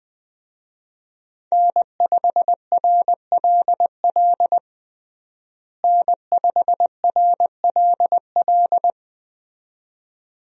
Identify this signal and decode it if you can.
Morse code, heard as a keyed tone, about 20 wpm, 700 Hz. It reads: N5RLL N5RLL
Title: USA Call Signs - 5 Characters - Repeated 2x - 20wpm